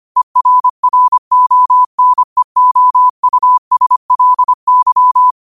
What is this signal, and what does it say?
Morse code, heard as a keyed tone, about 25 wpm, 1000 Hz. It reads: ERRONEOUSLY
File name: qotdfiles / 25wpm